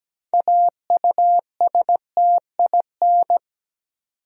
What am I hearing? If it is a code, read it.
Morse code, heard as a keyed tone, about 17 wpm, 700 Hz. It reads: AUSTIN
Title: USA Common Names - 17wpm